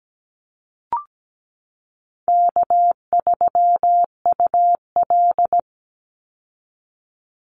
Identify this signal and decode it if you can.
Morse code, heard as a keyed tone, about 17 wpm, 700 Hz. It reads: K3UL
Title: Easy Call Signs Repeated 3x - 17wpm